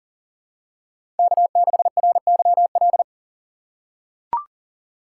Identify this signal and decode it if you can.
Morse code, heard as a keyed tone, about 37 wpm, 700 Hz. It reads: K6RYL
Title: USA Call Signs - 5 Characters - 4x Character Spacing - 40wpm